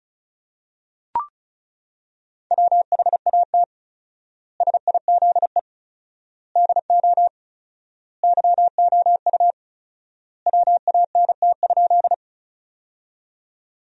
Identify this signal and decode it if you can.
Morse code, heard as a keyed tone, about 34 wpm, 700 Hz. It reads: WHAT SIZE DO YOU WANT?
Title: 4x Word-Spacing - Sentences from Top 400 Words - 35wpm